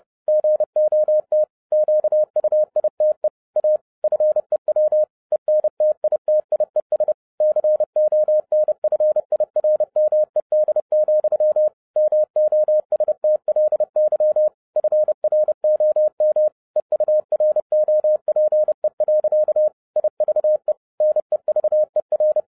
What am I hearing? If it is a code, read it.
Morse code, heard as a keyed tone, about 30 wpm, 610 Hz. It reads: GOT QUITE A FEW ENTITIES CONFIRMED, MOSTLY FROM EUROPE. IVE NEVER
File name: 2012-01-mp3 / n12-I DIDNT MISS THE SOLAR CYCLE 24-30wpm